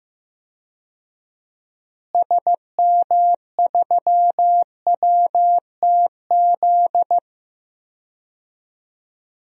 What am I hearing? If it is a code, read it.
Morse code, heard as a keyed tone, about 15 wpm, 700 Hz. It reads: SM3WTZ
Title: DXCC-100 Call Signs - 6 Characters - Repeated 2x - 15wpm